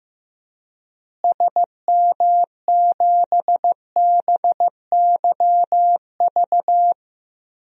Morse code, heard as a keyed tone, about 15 wpm, 700 Hz. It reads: SM7BYV
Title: DXCC-100 Call Signs - 6 Characters - Repeated 3x - 15wpm